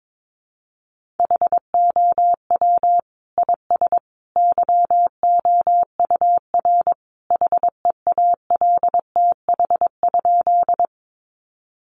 Morse code, heard as a keyed tone, about 22 wpm, 700 Hz. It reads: HOW IS YOUR HEALTH?
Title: Sentences from Top 600 Words 22wpm